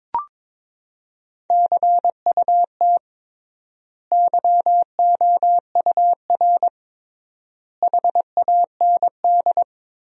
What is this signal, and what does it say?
Morse code, heard as a keyed tone, about 22 wpm, 700 Hz. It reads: CUT YOUR HAND
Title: 3x Word-Spacing - Sentences from Top 300 Words - 22wpm